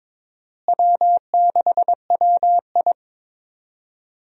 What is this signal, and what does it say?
Morse code, heard as a keyed tone, about 22 wpm, 700 Hz. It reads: W6WI
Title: USA Call Signs - 4 Characters - Repeated 2x - 22wpm